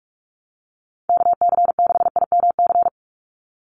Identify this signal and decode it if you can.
Morse code, heard as a keyed tone, about 44 wpm, 700 Hz. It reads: KC6IMC
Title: USA Call Signs 45wpm